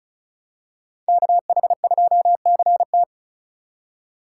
Morse code, heard as a keyed tone, about 36 wpm, 700 Hz. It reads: KH2CT
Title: USA Call Signs - 5 Characters - 2x Character Spacing - 35wpm